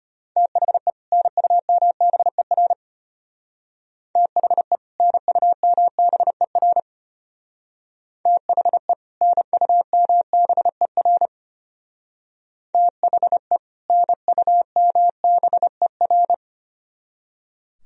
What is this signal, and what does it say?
Morse code, heard as a keyed tone, about 32 wpm, 700 Hz. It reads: THE NUMBER THE NUMBER THE NUMBER THE NUMBER